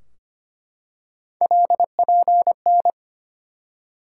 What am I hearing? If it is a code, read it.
Morse code, heard as a keyed tone, about 25 wpm, 700 Hz. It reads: LPN
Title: Practice QSO - Exchange 3 - More Info - 25wpm